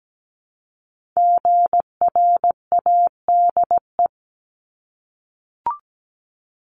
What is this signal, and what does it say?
Morse code, heard as a keyed tone, about 17 wpm, 700 Hz. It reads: GRADE
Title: Multi-letter Phonemes Repeated 3x 17wpm